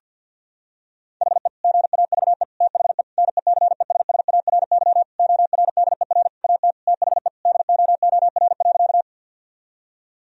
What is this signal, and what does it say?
Morse code, heard as a keyed tone, about 49 wpm, 700 Hz. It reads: HE GAVE THE NECESSARY ORDER AT THE DOOR.
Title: Sentences from Top 1000 Words - Review Part 2 - 50wpm